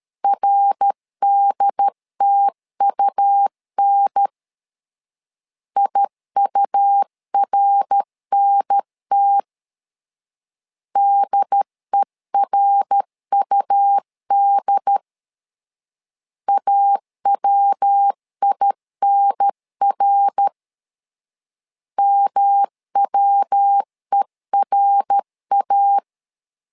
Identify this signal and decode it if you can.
Morse code, heard as a keyed tone, about 13 wpm, 800 Hz. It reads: RDTUN IURNT DERUD AWINR MWERA
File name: LEZIONI / 3 MWRO / 3 MWR 12wpm 1711